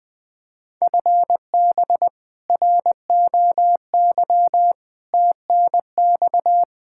Morse code, heard as a keyed tone, about 20 wpm, 700 Hz. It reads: FB ROY TNX
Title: K1USN SST QSO Practice 20wpm